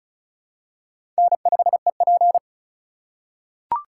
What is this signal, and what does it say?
Morse code, heard as a keyed tone, about 34 wpm, 700 Hz. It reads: N5EP
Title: USA Call Signs - 4 Characters - 4x Character Spacing - 35wpm